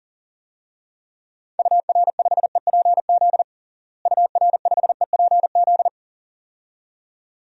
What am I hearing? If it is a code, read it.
Morse code, heard as a keyed tone, about 40 wpm, 700 Hz. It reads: UR5EPZ UR5EPZ
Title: DXCC-100 Call Signs - 6 Characters - Repeated 2x - 40wpm